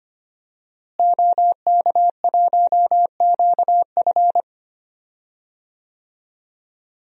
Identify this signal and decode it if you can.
Morse code, heard as a keyed tone, about 25 wpm, 700 Hz. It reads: OK1QF
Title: DXCC-100 Call Signs - 5 Characters - Repeated 2x - 25wpm